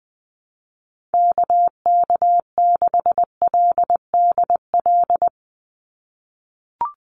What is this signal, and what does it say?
Morse code, heard as a keyed tone, about 19 wpm, 700 Hz. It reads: KK6LDL
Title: USA Call Signs 20wpm